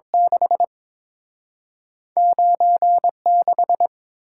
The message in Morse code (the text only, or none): 6 9 6